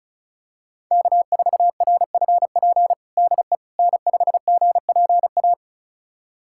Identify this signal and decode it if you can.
Morse code, heard as a keyed tone, about 36 wpm, 700 Hz. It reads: K4RFP DE N5GPA
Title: USA Callsign DE Callsign - 5 Characters - 35wpm